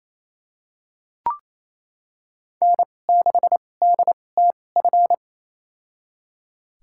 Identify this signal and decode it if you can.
Morse code, heard as a keyed tone, about 28 wpm, spaced as for 20 wpm, 700 Hz. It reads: N6DTF
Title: USA Call Signs - 2x Character Spacing - 28wpm